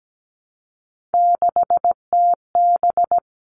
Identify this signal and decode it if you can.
Morse code, heard as a keyed tone, about 17 wpm, 700 Hz. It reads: 6TB
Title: All Cut Numbers - 3-Digit 17wpm